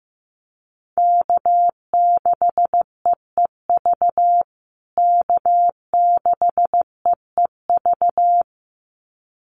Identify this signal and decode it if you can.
Morse code, heard as a keyed tone, about 15 wpm, 700 Hz. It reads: K6EEV K6EEV
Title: USA Call Signs Repeated 2x 15wpm